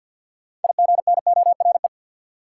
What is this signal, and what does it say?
Morse code, heard as a keyed tone, about 47 wpm, 700 Hz. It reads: IGNORE
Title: Six Letter Words - 50wpm